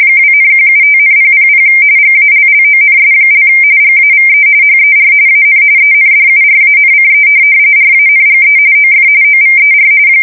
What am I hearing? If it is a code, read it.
RTTY, about 45.45 baud. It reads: 2,175,000 GALLONS) GAVE HER AN ENDURANCE RANGE OF OVER
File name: fsk170 / USS-New-Jersey